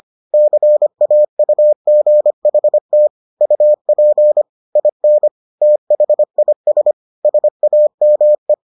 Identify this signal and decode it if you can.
Morse code, heard as a keyed tone, about 25 wpm, 600 Hz. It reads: CAUGHT UP IN THIS SAME